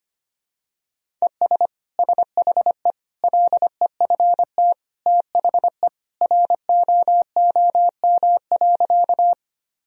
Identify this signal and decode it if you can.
Morse code, heard as a keyed tone, about 25 wpm, 700 Hz. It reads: ES SHE LEFT THE ROOM.
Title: Sentences from Top 300 Words and Abbreviations 25wpm